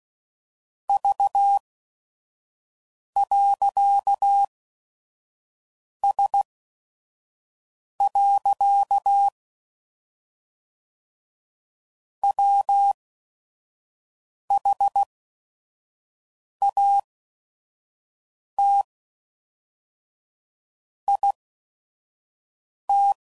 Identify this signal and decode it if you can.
Morse code, heard as a keyed tone, about 16 wpm, spaced as for 5 wpm, 800 Hz. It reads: V.S. WHAT IT
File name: CW Code Groups @5WPM / Technical Subjects / OR Gate